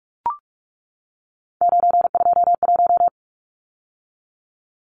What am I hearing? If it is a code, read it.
Morse code, heard as a keyed tone, about 44 wpm, 700 Hz. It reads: 921
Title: Numbers - 3-Digit 45wpm